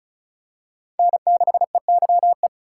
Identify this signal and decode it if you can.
Morse code, heard as a keyed tone, about 36 wpm, 700 Hz. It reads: N6EYE
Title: USA Call Signs - 4x Character Spacing - 35wpm